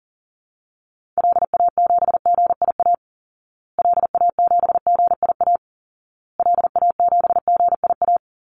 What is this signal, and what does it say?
Morse code, heard as a keyed tone, about 40 wpm, 700 Hz. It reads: LA7GIA LA7GIA LA7GIA
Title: Call Signs Repeated 2x - 40wpm